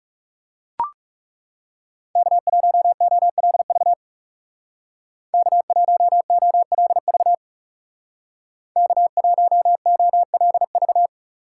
Morse code, heard as a keyed tone, about 36 wpm, 700 Hz. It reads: K1OLV K1OLV K1OLV
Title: Speed Racing - USA 1x3 Call Signs 30wpm